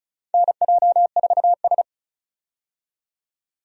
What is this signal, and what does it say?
Morse code, heard as a keyed tone, about 34 wpm, 700 Hz. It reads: NJ4S
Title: USA Call Signs - 4 Characters - 35wpm